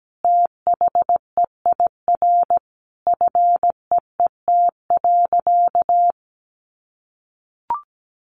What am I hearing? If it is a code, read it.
Morse code, heard as a keyed tone, about 17 wpm, 700 Hz. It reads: THEIR FEET.
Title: Sentences from Top 1000 Words 17wpm